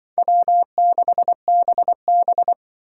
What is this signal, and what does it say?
Morse code, heard as a keyed tone, about 25 wpm, 700 Hz. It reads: W6BB
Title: Speed Racing - USA 1x2 Call Signs 20wpm